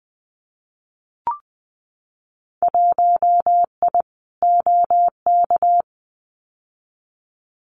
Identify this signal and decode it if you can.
Morse code, heard as a keyed tone, about 20 wpm, 700 Hz. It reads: 1I OK
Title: Winter Field Day Exchanges Encore 20wpm